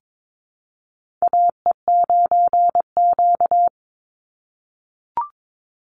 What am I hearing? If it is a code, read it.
Morse code, heard as a keyed tone, about 22 wpm, 700 Hz. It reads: AE9Q
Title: NA SOTA Chaser Callsigns 22wpm